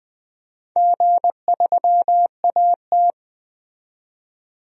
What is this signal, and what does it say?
Morse code, heard as a keyed tone, about 20 wpm, 700 Hz. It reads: G3AT
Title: DXCC-100 Call Signs - 4 Characters - 20wpm